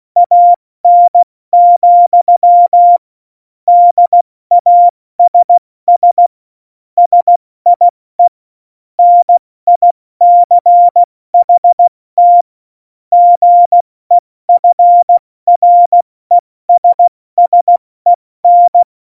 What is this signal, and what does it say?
Morse code, heard as a keyed tone, about 16 wpm, 700 Hz. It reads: AN, DASS SIE NICHT GEFRESSEN
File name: Morsekurs / Witze / Witz 12. Januar 2021, 70 F80